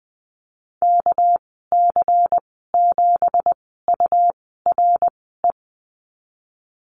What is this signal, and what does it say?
Morse code, heard as a keyed tone, about 20 wpm, spaced as for 15 wpm, 700 Hz. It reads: KC7URE
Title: 213 Lesson 34g - Number 8 - Call Signs 20wpm